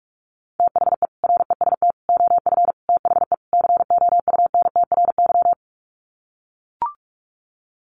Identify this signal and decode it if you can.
Morse code, heard as a keyed tone, about 44 wpm, 700 Hz. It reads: THE REST OF THE COUNTRY
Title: Sets of 5 Words 45wpm